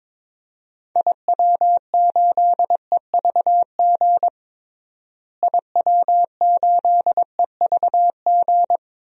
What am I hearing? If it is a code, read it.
Morse code, heard as a keyed tone, about 22 wpm, 700 Hz. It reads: IW8EVG IW8EVG
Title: DXCC-100 Call Signs - 6 Characters - Repeated 2x - 22wpm